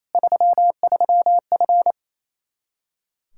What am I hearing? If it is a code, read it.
Morse code, heard as a keyed tone, about 28 wpm, 700 Hz. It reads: 33F
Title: Practice QSO - Exchange 2 - Info - 28wpm